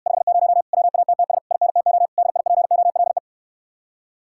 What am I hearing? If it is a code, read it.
Morse code, heard as a keyed tone, about 69 wpm, 700 Hz. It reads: 50 WATTS INTO DIPOLE